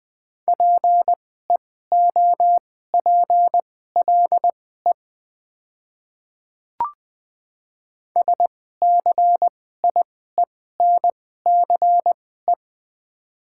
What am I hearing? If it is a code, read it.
Morse code, heard as a keyed tone, about 20 wpm, spaced as for 15 wpm, 700 Hz. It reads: PEOPLE SCIENCE
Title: 2x Character Spacing - Top 500 Words - 20wpm